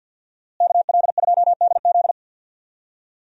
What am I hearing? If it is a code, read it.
Morse code, heard as a keyed tone, about 49 wpm, 700 Hz. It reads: KL2DZ
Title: USA Call Signs - 5 Characters - Repeated 2x - 50wpm